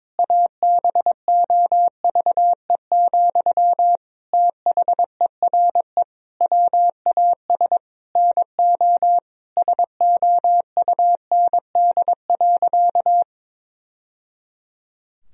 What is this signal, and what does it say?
Morse code, heard as a keyed tone, about 21 wpm, 700 Hz. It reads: ABOVE, THERE WAS NO SOUND.
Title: Sentences from Top 1000 Words - Review Part 2 - 22wpm